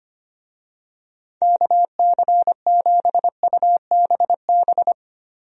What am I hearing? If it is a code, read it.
Morse code, heard as a keyed tone, about 25 wpm, 700 Hz. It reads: KC7UBB
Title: USA Call Signs - 6 Characters - Repeated 2x - 25wpm